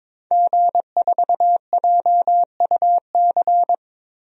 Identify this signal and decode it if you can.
Morse code, heard as a keyed tone, about 22 wpm, 700 Hz. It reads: G4JUC
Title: DXCC-100 Call Signs - 5 Characters - Repeated 3x - 22wpm